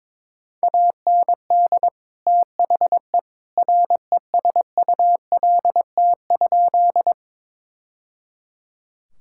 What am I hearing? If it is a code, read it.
Morse code, heard as a keyed tone, about 22 wpm, 700 Hz. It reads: AND THE RESULT?
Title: Sentences from Top 1000 Words - Review Part 5 - 22wpm